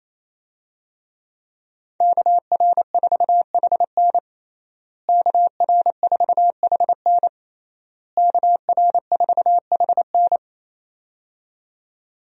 Morse code, heard as a keyed tone, about 27 wpm, 700 Hz. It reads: KR4HN KR4HN KR4HN
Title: USA Call Signs - 5 Characters - Repeated 3x - 28wpm